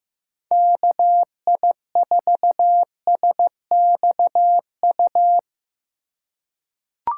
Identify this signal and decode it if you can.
Morse code, heard as a keyed tone, about 15 wpm, 700 Hz. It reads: KI4SXU
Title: USA Call Signs - 6 Characters - 3x Character Spacing - 15wpm